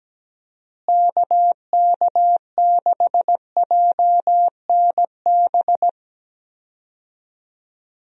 Morse code, heard as a keyed tone, about 17 wpm, 700 Hz. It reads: KK6JNB
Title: USA Call Signs - 6 Characters - Repeated 3x - 17wpm